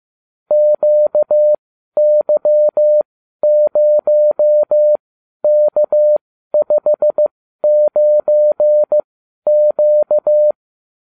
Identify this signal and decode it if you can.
Morse code, heard as a keyed tone, about 15 wpm, spaced as for 12 wpm, 600 Hz. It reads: QY0K59Q
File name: cw / ubacw30 3